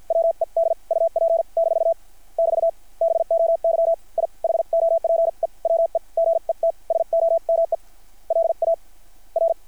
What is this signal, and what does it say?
Morse code, heard as a keyed tone, about 50 wpm, spaced as for 38 wpm, 660 Hz. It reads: WED UP- <BT> BOY I HOPE WE GET SOME LAR
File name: archived / CRSnet